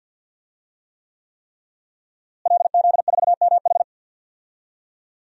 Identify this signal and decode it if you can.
Morse code, heard as a keyed tone, about 50 wpm, 700 Hz. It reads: LZ4MH
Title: DXCC-100 Call Signs - 5 Characters - Repeated 3x - 50wpm